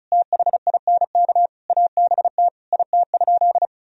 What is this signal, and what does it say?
Morse code, heard as a keyed tone, about 36 wpm, 700 Hz. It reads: THINK ABT IT?
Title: Sentences from Top 200 Words and Abbreviations 35wpm